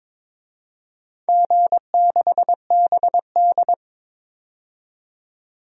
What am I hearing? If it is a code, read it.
Morse code, heard as a keyed tone, about 22 wpm, 700 Hz. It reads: G6BD
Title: DXCC-100 Call Signs - 4 Characters - 22wpm